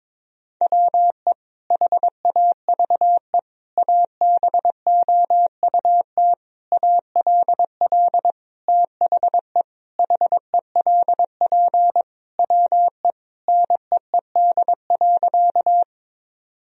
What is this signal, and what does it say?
Morse code, heard as a keyed tone, about 22 wpm, 700 Hz. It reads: WE HAVE ABOUT ALL THE HELP WE NEED.
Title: Sentences from Top 1000 Words - Review Part 1 - 22wpm